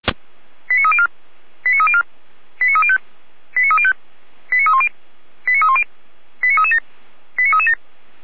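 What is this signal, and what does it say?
Five-tone selective call (ZVEI), heard as quick runs of five tones, tones of about 70 ms. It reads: 89304 89304 89305 89305 89310 89310 89337 89337
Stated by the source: ZVEI five-tone selective call received by SDR.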